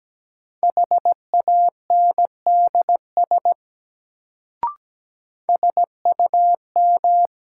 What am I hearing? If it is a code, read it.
Morse code, heard as a keyed tone, about 17 wpm, 700 Hz. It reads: HANDS SUM
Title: Sentences from Top 1000 Words - Review Part 6 - New Word Drill - 17wpm